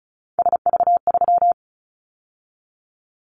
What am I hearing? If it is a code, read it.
Morse code, heard as a keyed tone, about 36 wpm, 700 Hz. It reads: SV3
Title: Callsign Prefixes for Top 50 DXCC - 35wpm